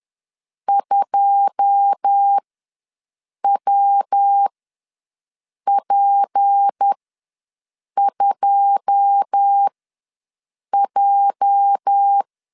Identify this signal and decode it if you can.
Morse code, heard as a keyed tone, about 10 wpm, spaced as for 6 wpm, 800 Hz. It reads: 2WP2J
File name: LEZIONI / 9 2 ws 10wpm 4130